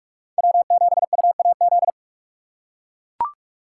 Morse code, heard as a keyed tone, about 43 wpm, 700 Hz. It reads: W7UAZ